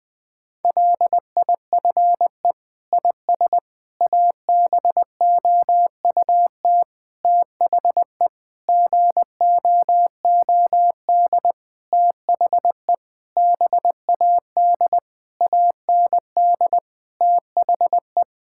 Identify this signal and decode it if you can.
Morse code, heard as a keyed tone, about 20 wpm, 700 Hz. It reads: LIFE IS ABOUT THE GOOD THE BAD AND THE
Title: Sentences with 6-Word Phrases - Spoken Beginning - 20wpm